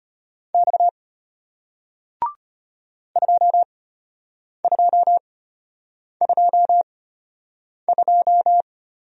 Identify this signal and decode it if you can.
Morse code, heard as a keyed tone, about 33 wpm, spaced as for 8 wpm, 700 Hz. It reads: X 2222